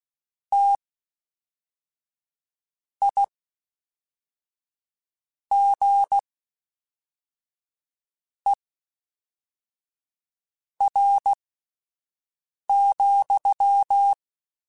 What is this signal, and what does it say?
Morse code, heard as a keyed tone, about 16 wpm, spaced as for 4 wpm, 800 Hz. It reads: TIGER,
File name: Misc / Misc-mp3s / WordPairs sp @5